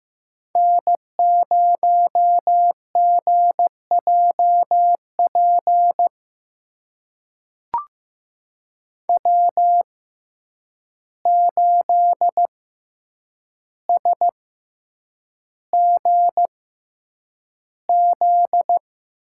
Morse code, heard as a keyed tone, about 15 wpm, spaced as for 8 wpm, 700 Hz. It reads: N0GJP W8SGZ
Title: USA Call Signs - 5 Characters - 6x Character Spacing - 15wpm